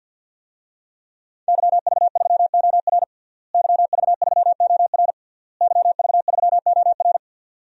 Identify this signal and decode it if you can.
Morse code, heard as a keyed tone, about 49 wpm, 700 Hz. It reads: YV3OR YV3OR YV3OR
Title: DXCC-100 Call Signs - 5 Characters - Repeated 3x - 50wpm